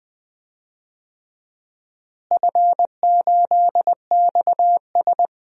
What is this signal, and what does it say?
Morse code, heard as a keyed tone, about 20 wpm, 700 Hz. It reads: F8XS
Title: DXCC-100 Call Signs - 4 Characters - Repeated 3x - 20wpm